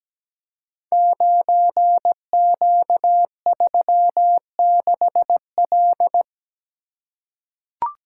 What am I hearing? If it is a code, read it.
Morse code, heard as a keyed tone, about 17 wpm, 700 Hz. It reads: 9Q36L
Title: Code Groups - 5 Letter:Numbers - 17wpm